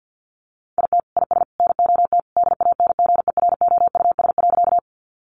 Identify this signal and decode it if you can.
Morse code, heard as a keyed tone, about 50 wpm, 700 Hz. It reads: IT IS NOT DANGEROUS.
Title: Sentences from Top 1200 Words 50wpm